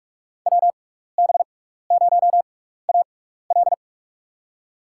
Morse code, heard as a keyed tone, about 46 wpm, spaced as for 16 wpm, 700 Hz. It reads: WB0AL
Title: USA Call Signs - 5 Characters - 6x Character Spacing - 45wpm